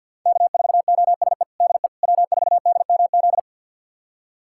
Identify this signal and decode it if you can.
Morse code, heard as a keyed tone, about 51 wpm, 700 Hz. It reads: K4OSE DE W4DMZ
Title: USA Callsign DE Callsign - 5 Characters - 50wpm